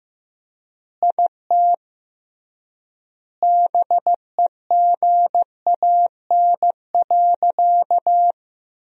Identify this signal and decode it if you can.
Morse code, heard as a keyed tone, about 15 wpm, 700 Hz. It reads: IT BEGAN.
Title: 3x Word-Spacing - Sentences from Top 400 Words - 15wpm